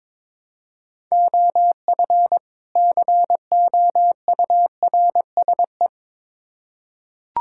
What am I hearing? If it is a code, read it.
Morse code, heard as a keyed tone, about 22 wpm, 700 Hz. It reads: OF COURSE
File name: Build A Phrase - 2-Word Phrases - 22wpm